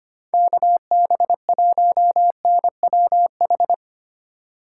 Morse code, heard as a keyed tone, about 25 wpm, 700 Hz. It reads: KB1NWH
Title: USA Call Signs - 3x Character Spacing - 25wpm